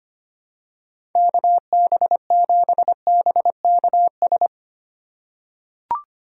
Morse code, heard as a keyed tone, about 25 wpm, 700 Hz. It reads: KB7BKS
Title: USA Call Signs - 2x Character Spacing - 25wpm